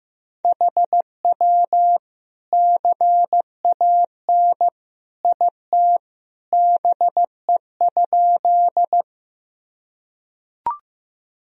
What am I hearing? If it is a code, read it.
Morse code, heard as a keyed tone, about 15 wpm, 700 Hz. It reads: HW CAN IT BE?
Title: Sentences from Top 100 Words and Abbreviations 15wpm